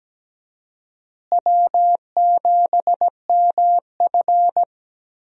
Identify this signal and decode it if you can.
Morse code, heard as a keyed tone, about 17 wpm, 700 Hz. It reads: W7MF
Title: USA Call Signs - 4 Characters - Repeated 2x - 17wpm